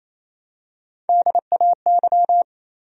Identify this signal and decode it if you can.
Morse code, heard as a keyed tone, about 28 wpm, 700 Hz. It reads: DAY
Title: Top 100 Words 28wpm